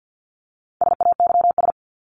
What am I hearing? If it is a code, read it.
Morse code, heard as a keyed tone, about 50 wpm, 700 Hz. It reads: SAYS
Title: ES:S Suffix - Top 100 Words with Base Words 50wpm